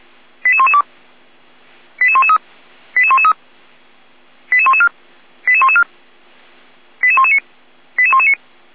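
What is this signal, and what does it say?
Five-tone selective call (ZVEI), heard as quick runs of five tones, tones of about 70 ms. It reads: 88112 88113 88113 88114 88114 88119 88119
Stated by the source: ZVEI five-tone call 88111 received by SDR.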